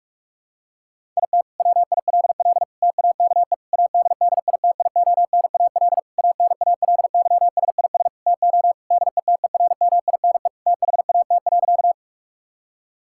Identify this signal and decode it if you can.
Morse code, heard as a keyed tone, about 46 wpm, 700 Hz. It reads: IT WILL TAKE ADDITIONAL ANALYSIS TO DETERMINE THAT.
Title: Sentences from Top 1000 Words - Review Part 5 - 45wpm